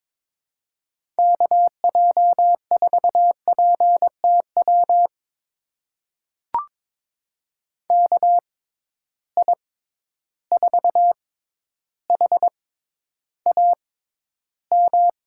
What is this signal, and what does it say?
Morse code, heard as a keyed tone, about 22 wpm, spaced as for 11 wpm, 700 Hz. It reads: KJ4PTW KI4HAM
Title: USA Call Signs - 6x Character Spacing - 22wpm